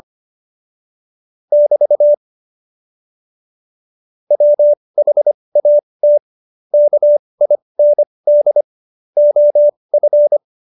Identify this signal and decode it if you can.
Morse code, heard as a keyed tone, about 25 wpm, spaced as for 20 wpm, 590 Hz. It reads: <BT> WHAT KIND OF
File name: Jokes-25-char-20-eff